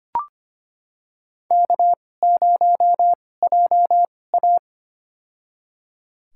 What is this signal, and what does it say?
Morse code, heard as a keyed tone, about 25 wpm, spaced as for 18 wpm, 700 Hz. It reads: K0JA
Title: USA Call Signs - 4 Characters - 2x Character Spacing - 25wpm